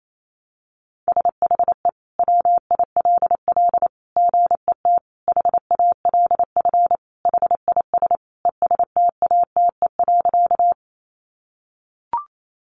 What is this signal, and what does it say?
Morse code, heard as a keyed tone, about 28 wpm, 700 Hz. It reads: SHE WILL GET HALF HIS ESTATE.
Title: Sentences from Top 1200 Words 28wpm